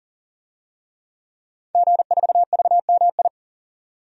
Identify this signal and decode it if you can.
Morse code, heard as a keyed tone, about 40 wpm, 700 Hz. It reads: G4VMI
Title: DXCC-100 Call Signs - 5 Characters - 40wpm